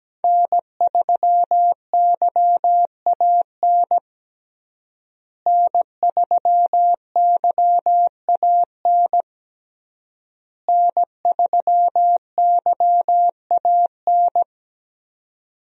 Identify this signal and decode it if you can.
Morse code, heard as a keyed tone, about 17 wpm, 700 Hz. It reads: N3YAN N3YAN N3YAN
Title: USA Call Signs - 5 Characters - Repeated 3x - 17wpm